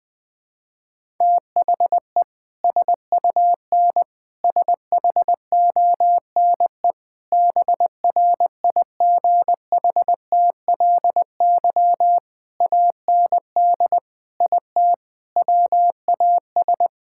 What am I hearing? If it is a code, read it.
Morse code, heard as a keyed tone, about 20 wpm, 700 Hz. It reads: THE SUN SHONE BRIGHTLY AND IT WAS
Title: Sentences with 2-Word Phrases - 20wpm